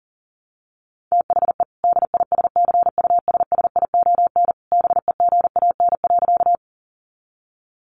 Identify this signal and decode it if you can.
Morse code, heard as a keyed tone, about 40 wpm, 700 Hz. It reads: THE DISCUSSION BEGAN.
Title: Sentences from Top 1100 Words 40wpm